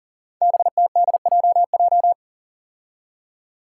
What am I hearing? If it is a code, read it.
Morse code, heard as a keyed tone, about 40 wpm, 700 Hz. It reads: BTDJJ
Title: Five Challenging Letter-Number Combinations 40wpm